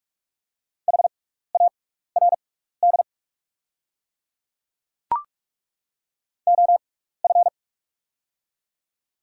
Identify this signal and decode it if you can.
Morse code, heard as a keyed tone, about 42 wpm, spaced as for 15 wpm, 690 Hz. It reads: HARD OF
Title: 6x Character Spacing - Top 500 Words - 45wpm